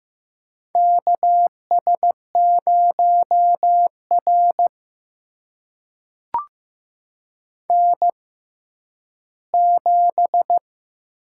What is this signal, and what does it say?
Morse code, heard as a keyed tone, about 15 wpm, spaced as for 10 wpm, 700 Hz. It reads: KS0R N7
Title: USA Call Signs - 6x Character Spacing - 15wpm